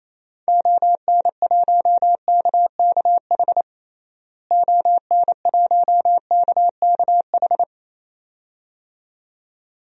Morse code, heard as a keyed tone, about 28 wpm, 700 Hz. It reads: ON1KKH ON1KKH
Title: DXCC-100 Call Signs - 6 Characters - Repeated 2x - 28wpm